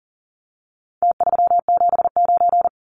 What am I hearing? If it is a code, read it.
Morse code, heard as a keyed tone, about 40 wpm, 700 Hz. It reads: T379
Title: Numbers - Contest Style - 4-Digit 40wpm